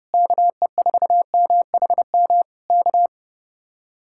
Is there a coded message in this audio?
Yes: Morse code from 0.1 to 3.1 s, heard as a keyed tone, about 32 wpm, 700 Hz. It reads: KE4MHM K